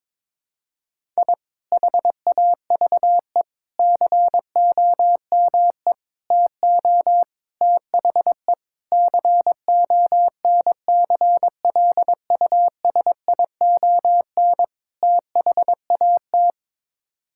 Morse code, heard as a keyed tone, about 21 wpm, 700 Hz. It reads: I HAVE COME TO THE CONCLUSION THAT
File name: practice / 22wpm / Sets of 7 Words - 22wpm